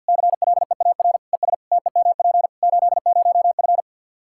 Morse code, heard as a keyed tone, about 48 wpm, 700 Hz. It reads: CLEAR ES TEMP 80F
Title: K1USN SST QSO Practice 50wpm